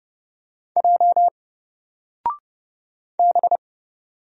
Morse code, heard as a keyed tone, about 30 wpm, 700 Hz. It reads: J B
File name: Single-Letter-Number-30wpm